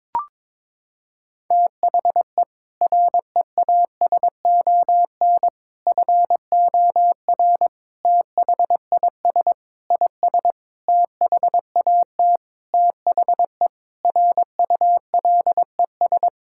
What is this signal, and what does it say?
Morse code, heard as a keyed tone, about 22 wpm, 700 Hz. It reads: THE REASON FOR THIS IS THAT THE RULES
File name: Sentences with 7-Word Phrases - 22wpm